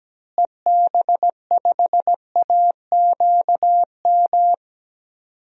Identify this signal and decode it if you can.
Morse code, heard as a keyed tone, about 17 wpm, 700 Hz. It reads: EB5AQM
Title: DXCC-100 Call Signs - 6 Characters - Repeated 3x - 17wpm